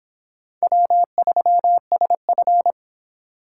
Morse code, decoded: W3SF

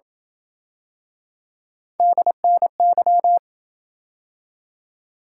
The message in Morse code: DNY

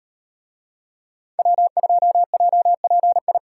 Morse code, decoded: W2JPI

